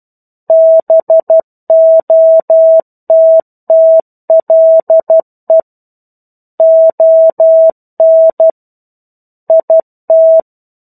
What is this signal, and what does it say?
Morse code, heard as a keyed tone, about 12 wpm, 650 Hz. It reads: BOTTLE ON IT
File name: 01-Chapter-01 12 WPM